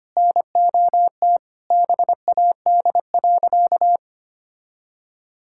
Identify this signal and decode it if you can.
Morse code, heard as a keyed tone, about 25 wpm, 700 Hz. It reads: NOT BAD.
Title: Sentences from Top 1000 Words - Review Part 5 - 25wpm